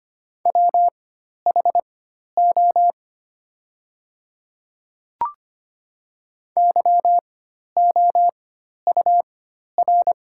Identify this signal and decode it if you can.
Morse code, heard as a keyed tone, about 25 wpm, spaced as for 12 wpm, 700 Hz. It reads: WHO YOUR
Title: 4x Character Spacing - Top 100 Words - 25wpm